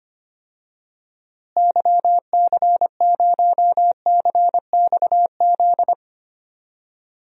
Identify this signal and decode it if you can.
Morse code, heard as a keyed tone, about 25 wpm, 700 Hz. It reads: YC0CXZ
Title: DXCC-100 Call Signs - 6 Characters - Repeated 2x - 25wpm